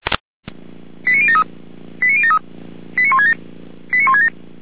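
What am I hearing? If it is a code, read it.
Five-tone selective call (ZVEI), heard as quick runs of five tones, tones of about 70 ms. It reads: 89073 89073 89167 89167
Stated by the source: ZVEI five-tone selective call received via SDR.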